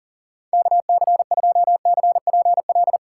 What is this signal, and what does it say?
Morse code, heard as a keyed tone, about 40 wpm, 700 Hz. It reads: KC2CPL
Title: USA Call Signs - 6 Characters - 6x Character Spacing - 40wpm